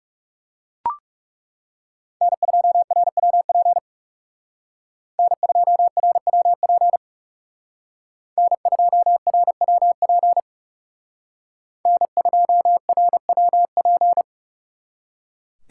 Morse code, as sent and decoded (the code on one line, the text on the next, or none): -. ..--- .-. .-- .--. / -. ..--- .-. .-- .--. / -. ..--- .-. .-- .--. / -. ..--- .-. .-- .--.
N2RWP N2RWP N2RWP N2RWP